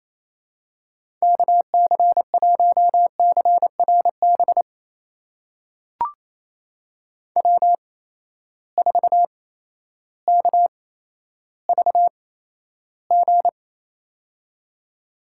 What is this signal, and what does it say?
Morse code, heard as a keyed tone, about 28 wpm, spaced as for 13 wpm, 700 Hz. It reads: KC1CRB W4KVG
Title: USA Call Signs - 8x Character Spacing - 28wpm